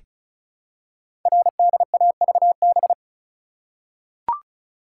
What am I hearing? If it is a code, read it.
Morse code, heard as a keyed tone, about 36 wpm, 700 Hz. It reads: RDAVB
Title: Code Groups - 5 Letters - 35wpm